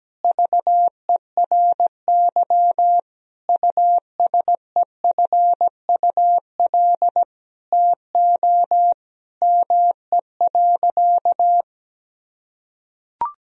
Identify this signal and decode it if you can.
Morse code, heard as a keyed tone, about 17 wpm, 700 Hz. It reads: VERY USEFUL TO ME.
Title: Sentences from Top 1000 Words - Review Part 4 - 17wpm